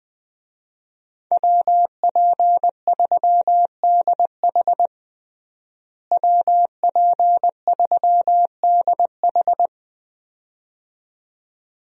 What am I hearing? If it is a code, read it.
Morse code, heard as a keyed tone, about 21 wpm, 700 Hz. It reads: WP3DH WP3DH
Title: USA Call Signs - 5 Characters - Repeated 2x - 20wpm